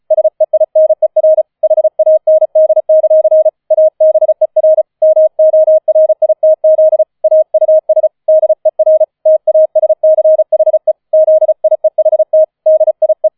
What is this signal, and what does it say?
Morse code, heard as a keyed tone, about 36 wpm, 630 Hz. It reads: SEINER HAND; ABER MORITZ AUS DER TASCHE ZIEHT DIE
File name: 004 Vierter Streich 35wpm